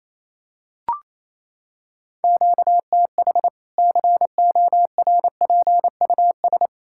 Morse code, heard as a keyed tone, about 28 wpm, 700 Hz. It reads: QTH CORPUS
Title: Practice QSO - Exchange 1 - Beginning - 28wpm